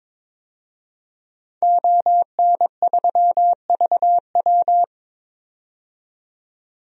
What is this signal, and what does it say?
Morse code, heard as a keyed tone, about 22 wpm, 700 Hz. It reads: ON3VW
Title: DXCC-100 Call Signs - 5 Characters - 22wpm